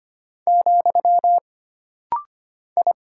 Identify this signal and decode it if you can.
Morse code, heard as a keyed tone, about 25 wpm, 700 Hz. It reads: , I
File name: Single Character (Letter:Number:Punctuation:Prosign) - Rapid-Fire - 25wpm